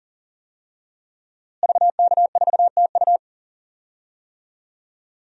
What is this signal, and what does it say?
Morse code, heard as a keyed tone, about 37 wpm, 700 Hz. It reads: VK4TU
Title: DXCC-100 Call Signs - 5 Characters - Repeated 3x - 40wpm